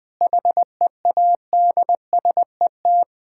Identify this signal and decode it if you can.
Morse code, heard as a keyed tone, about 21 wpm, 690 Hz. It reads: HEADSET
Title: Difficult Words with Hints - 20wpm